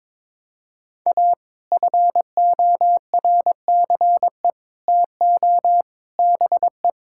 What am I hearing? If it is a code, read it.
Morse code, heard as a keyed tone, about 22 wpm, 700 Hz. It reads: A FORCE TO BE
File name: practice / 22wpm / Sets of 8 Words - 22wpm